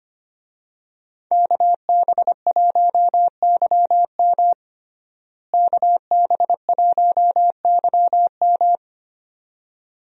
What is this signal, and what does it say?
Morse code, heard as a keyed tone, about 25 wpm, 700 Hz. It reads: KB1YM KB1YM
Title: USA Call Signs - 5 Characters - Repeated 2x - 25wpm